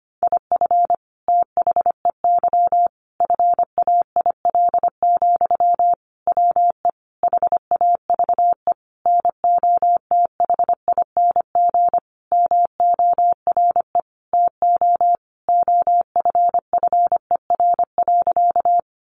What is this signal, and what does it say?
Morse code, heard as a keyed tone, about 25 wpm, 700 Hz. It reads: IF THEY FAIL, WE HAVE NOTHING MORE TO OFFER.
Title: Sentences from Top 1100 Words 25wpm